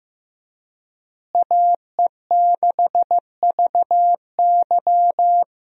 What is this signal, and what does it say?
Morse code, heard as a keyed tone, about 15 wpm, 700 Hz. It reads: AE6VY
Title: USA Call Signs - 5 Characters - Repeated 3x - 15wpm